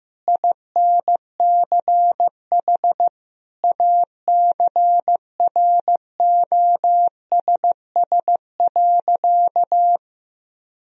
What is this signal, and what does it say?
Morse code, heard as a keyed tone, about 15 wpm, 700 Hz. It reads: INCH ACROSS.
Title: Sentences from Top 1000 Words - Review Part 3 - 15wpm